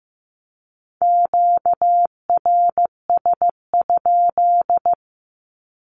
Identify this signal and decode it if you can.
Morse code, heard as a keyed tone, about 15 wpm, 700 Hz. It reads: QRS?
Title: Q-Codes 15wpm